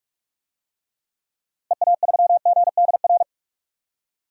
Morse code, heard as a keyed tone, about 41 wpm, 700 Hz. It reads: EA3GDR
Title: DXCC-100 Call Signs - 6 Characters - Repeated 2x - 45wpm